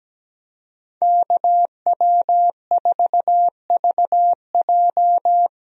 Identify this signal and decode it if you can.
Morse code, heard as a keyed tone, about 17 wpm, 700 Hz. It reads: KW4VJ
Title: USA Call Signs - 5 Characters - Repeated 3x - 17wpm